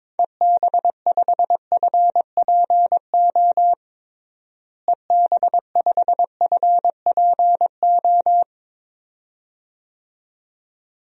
Morse code, heard as a keyed tone, about 22 wpm, 700 Hz. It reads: EB5FPO EB5FPO
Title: DXCC-100 Call Signs - 6 Characters - Repeated 2x - 22wpm